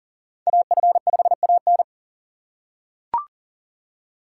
Morse code, heard as a keyed tone, about 39 wpm, 700 Hz. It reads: AF5AN
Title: USA Call Signs - 5 Characters - 3x Character Spacing - 40wpm